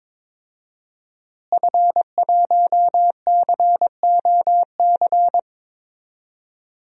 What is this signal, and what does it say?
Morse code, heard as a keyed tone, about 22 wpm, 700 Hz. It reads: F1COC
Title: DXCC-100 Call Signs - 5 Characters - Repeated 2x - 22wpm